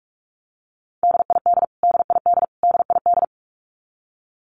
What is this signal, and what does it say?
Morse code, heard as a keyed tone, about 44 wpm, 700 Hz. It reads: DID DID DID
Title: Top 500 Words Repeated 45wpm